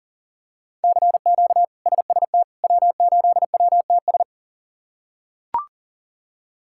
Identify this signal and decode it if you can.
Morse code, heard as a keyed tone, about 37 wpm, 700 Hz. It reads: CQ SST W8WTS
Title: K1USN SST QSO Practice - 40wpm